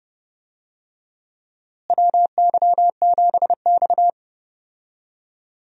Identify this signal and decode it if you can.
Morse code, heard as a keyed tone, about 30 wpm, 700 Hz. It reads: WY7X
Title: USA Call Signs - 4 Characters - Repeated 3x - 30wpm